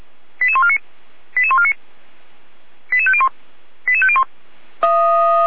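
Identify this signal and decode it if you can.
Five-tone selective call (ZVEI), heard as quick runs of five tones, tones of about 70 ms. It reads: 88149 88149 88501 88501
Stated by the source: Five-tone ZVEI selective call received on SDR.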